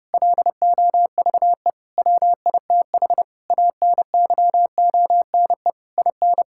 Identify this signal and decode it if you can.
Morse code, heard as a keyed tone, about 32 wpm, 700 Hz. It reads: LOVE WITH ANYONE IN
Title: Sentences from Top 1000 Words - Review Part 5 - 30wpm